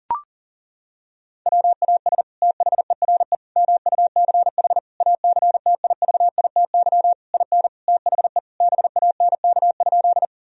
Morse code, heard as a keyed tone, about 40 wpm, 700 Hz. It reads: WAS THERE MUCH ACTIVITY IN THE BANK?
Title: Sentences from Top 1000 Words - Review Part 4 - 40wpm